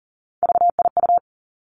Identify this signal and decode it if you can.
Morse code, heard as a keyed tone, about 38 wpm, 700 Hz. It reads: VIU
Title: Three Challenging Letter-Number Combinations 40wpm